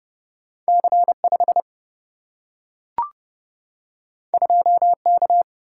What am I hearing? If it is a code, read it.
Morse code, heard as a keyed tone, about 30 wpm, 700 Hz. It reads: C5 2K
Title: Single-Letter Single-Number 30wpm